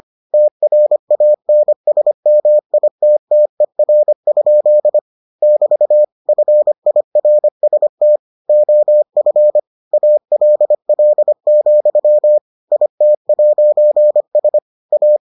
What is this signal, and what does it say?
Morse code, heard as a keyed tone, about 25 wpm, 600 Hz. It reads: TRANSMITTER? <BT> FIRST OF ALL, IT'S A